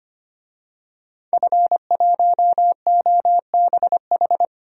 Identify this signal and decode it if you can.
Morse code, heard as a keyed tone, about 25 wpm, 700 Hz. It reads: F1OBH
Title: DXCC-100 Call Signs - 5 Characters - 25wpm